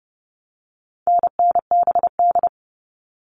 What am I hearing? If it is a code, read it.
Morse code, heard as a keyed tone, about 30 wpm, 700 Hz. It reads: NNBD
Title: All Cut Numbers - 4-Digit 30wpm